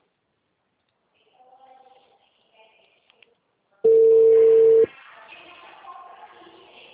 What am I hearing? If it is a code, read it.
Ringback tone (China, 450 Hz).